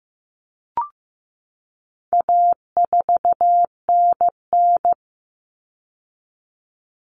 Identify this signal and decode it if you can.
Morse code, heard as a keyed tone, about 15 wpm, 700 Hz. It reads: A4NN
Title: Common Cut Numbers - 4-Digit 15wpm